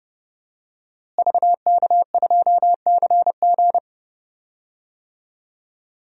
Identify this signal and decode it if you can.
Morse code, heard as a keyed tone, about 31 wpm, 700 Hz. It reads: VK2CG
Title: DXCC-100 Call Signs - 5 Characters - Repeated 2x - 30wpm